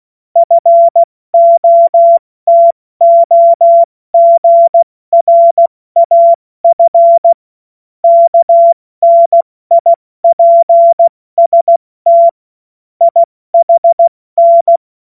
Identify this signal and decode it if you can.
Morse code, heard as a keyed tone, about 16 wpm, 680 Hz. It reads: FOTOGRAF KNIPST IHN